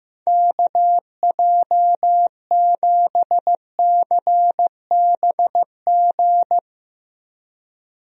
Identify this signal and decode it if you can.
Morse code, heard as a keyed tone, about 16 wpm, 700 Hz. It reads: KJ7CBG
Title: USA Call Signs - 6 Characters - Repeated 2x - 15wpm